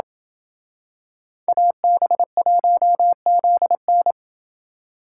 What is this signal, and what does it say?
Morse code, heard as a keyed tone, about 25 wpm, 700 Hz. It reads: AB1ZN
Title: Speed Racing - USA 2x2 Call Signs 20wpm